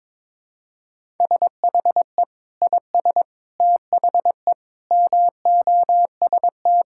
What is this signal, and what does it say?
Morse code, heard as a keyed tone, about 21 wpm, 700 Hz. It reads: SHE IS THE MOST
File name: practice / 22wpm / Sentences with 2-Word Phrases - 22wpm